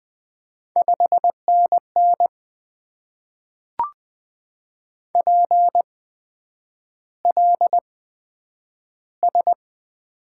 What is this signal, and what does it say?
Morse code, heard as a keyed tone, about 20 wpm, spaced as for 9 wpm, 700 Hz. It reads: 5NN PLS
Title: Top 90 QSO Elements - 8x Character Spacing - 20wpm